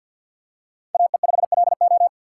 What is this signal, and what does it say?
Morse code, heard as a keyed tone, about 52 wpm, 700 Hz. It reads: AE5LO
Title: USA Call Signs - 5 Characters - 2x Character Spacing - 50wpm